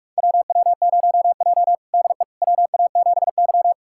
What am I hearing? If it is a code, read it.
Morse code, heard as a keyed tone, about 46 wpm, 700 Hz. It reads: WW0J DE WA7Y
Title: USA Callsign DE Callsign - 4 Characters - 45wpm